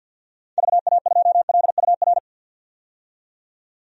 Morse code, heard as a keyed tone, about 52 wpm, 700 Hz. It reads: VA2LUR
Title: DXCC-100 Call Signs - 6 Characters - 50wpm